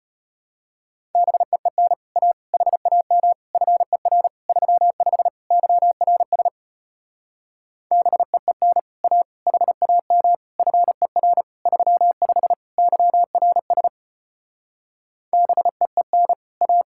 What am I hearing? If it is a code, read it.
Morse code, heard as a keyed tone, about 33 wpm, 700 Hz. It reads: BEEN A HAM FER 35 YRS BEEN A HAM FER 35 YRS BEEN A